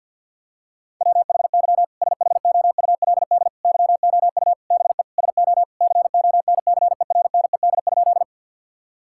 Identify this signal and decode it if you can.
Morse code, heard as a keyed tone, about 50 wpm, 700 Hz. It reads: WHY SHOULD YOU BE SO CONCERNED?
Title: Sentences from Top 1000 Words - Review Part 6 - 50wpm